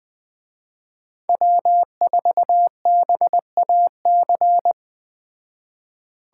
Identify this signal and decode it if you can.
Morse code, heard as a keyed tone, about 20 wpm, 700 Hz. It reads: W4BAC
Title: Speed Racing - USA Call Signs 17wpm